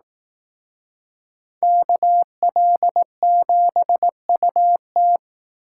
Morse code, heard as a keyed tone, about 18 wpm, 700 Hz. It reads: KL7UT
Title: Speed Racing - USA 2x2 Call Signs 15wpm